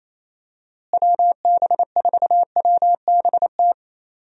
Morse code, heard as a keyed tone, about 28 wpm, 700 Hz. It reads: WB4WBT
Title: USA Call Signs - 6 Characters - Repeated 3x - 28wpm